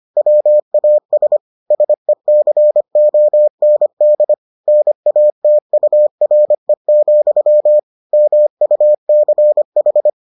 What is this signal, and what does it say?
Morse code, heard as a keyed tone, about 25 wpm, 600 Hz. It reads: WAS SECOND NATURE, MUCH